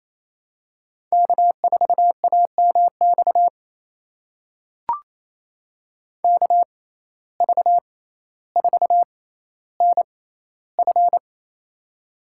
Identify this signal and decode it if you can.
Morse code, heard as a keyed tone, about 28 wpm, spaced as for 14 wpm, 700 Hz. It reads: K4AMX KV4NF
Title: USA Call Signs - 6x Character Spacing - 28wpm